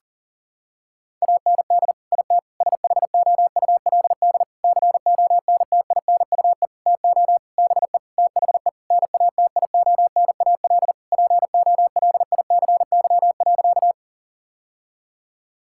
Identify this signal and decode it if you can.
Morse code, heard as a keyed tone, about 40 wpm, 700 Hz. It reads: AND IT SHOULD CONTINUE TO BE THE NATIONAL POLICY.
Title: Sentences from Top 1000 Words - Review Part 6 - 40wpm